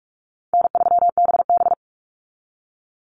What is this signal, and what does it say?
Morse code, heard as a keyed tone, about 44 wpm, 700 Hz. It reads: N3BB
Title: Easy Call Signs Repeated 3x - 45wpm